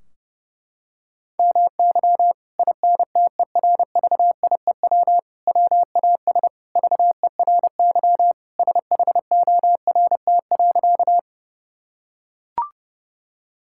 Morse code, heard as a keyed tone, about 30 wpm, 700 Hz. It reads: MY INTERVIEW WAS VERY SHORT.
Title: Sentences from Top 1600 Words 30wpm